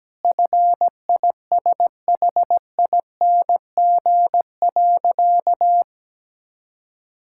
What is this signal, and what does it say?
Morse code, heard as a keyed tone, about 17 wpm, 700 Hz. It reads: FISHING.
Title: Sentences from Top 1000 Words - Review Part 5 - 17wpm